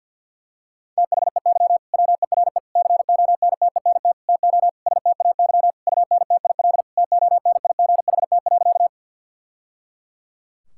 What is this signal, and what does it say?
Morse code, heard as a keyed tone, about 50 wpm, 700 Hz. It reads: THEY WERE CONTENT TO STAY UNTIL TONIGHT.
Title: Sentences from Top 1000 Words - Review Part 6 - 50wpm